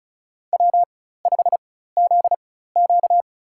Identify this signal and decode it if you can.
Morse code, heard as a keyed tone, about 34 wpm, spaced as for 16 wpm, 700 Hz. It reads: W5ZQ
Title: USA Call Signs - 4 Characters - 4x Character Spacing - 35wpm